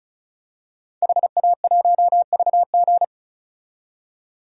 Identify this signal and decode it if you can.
Morse code, heard as a keyed tone, about 36 wpm, 700 Hz. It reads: HA1VG
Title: DXCC-100 Call Signs - 5 Characters - 35wpm